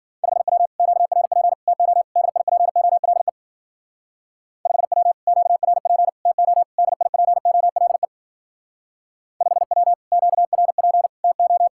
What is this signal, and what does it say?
Morse code, heard as a keyed tone, about 47 wpm, 700 Hz. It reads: 5W QRP TO DIPOLE 5W QRP TO DIPOLE 5W QRP TO
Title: Speed-Racing Common QSO Phrases Encore - 40wpm